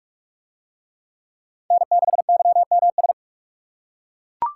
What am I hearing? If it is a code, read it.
Morse code, heard as a keyed tone, about 40 wpm, 700 Hz. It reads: N6YMS